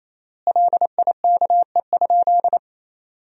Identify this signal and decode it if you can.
Morse code, heard as a keyed tone, about 28 wpm, 700 Hz. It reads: LIKE?
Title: 4x Word-Spacing - Sentences from Top 100 Words - 28wpm